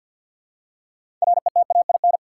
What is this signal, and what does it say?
Morse code, heard as a keyed tone, about 53 wpm, 700 Hz. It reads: RETAIN
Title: Six Letter Words - 50wpm